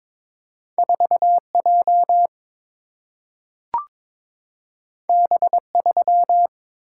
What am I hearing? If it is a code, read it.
Morse code, heard as a keyed tone, about 22 wpm, 700 Hz. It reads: 4 J B 3
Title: Single-Letter Single-Number 22wpm